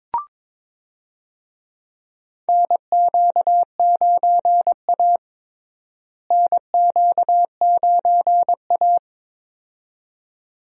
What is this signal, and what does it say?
Morse code, heard as a keyed tone, about 22 wpm, 700 Hz. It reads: NQ9A NQ9A
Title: USA Call Signs - 4 Characters - Repeated 2x - 22wpm